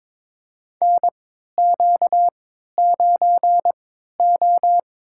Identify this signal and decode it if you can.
Morse code, heard as a keyed tone, about 22 wpm, spaced as for 13 wpm, 700 Hz. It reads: NQ9O
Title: USA Call Signs - 4 Characters - 3x Character Spacing - 22wpm